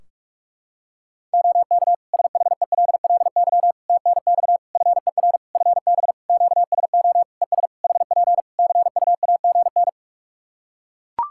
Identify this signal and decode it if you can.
Morse code, heard as a keyed tone, about 41 wpm, 700 Hz. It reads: OK SHELLY TNX FER FB QSO ES HP CUAGN
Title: Practice QSO - Exchange 4 - Ending - 45wpm